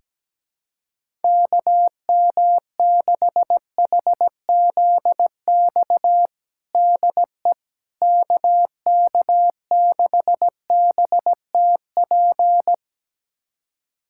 Morse code, heard as a keyed tone, about 17 wpm, 700 Hz. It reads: KM6HZX DE KK6BTP
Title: USA Callsign DE Callsign 17wpm